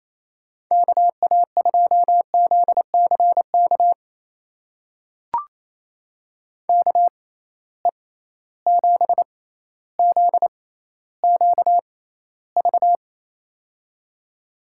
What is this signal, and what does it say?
Morse code, heard as a keyed tone, about 27 wpm, spaced as for 14 wpm, 700 Hz. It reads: KA2ZCK KE7ZQV
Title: USA Call Signs - 6 Characters - 6x Character Spacing - 28wpm